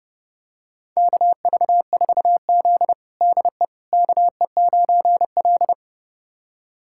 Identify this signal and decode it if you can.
Morse code, heard as a keyed tone, about 30 wpm, 700 Hz. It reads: KV4Z DE KE9L
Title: USA Callsign DE Callsign - 4 Characters - 30wpm